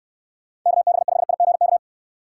Morse code, heard as a keyed tone, about 66 wpm, 700 Hz. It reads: KB5IGK